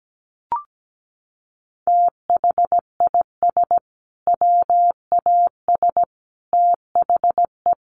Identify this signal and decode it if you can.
Morse code, heard as a keyed tone, about 17 wpm, 700 Hz. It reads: THIS WAS THE
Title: Sentences from Top 900 Words 17wpm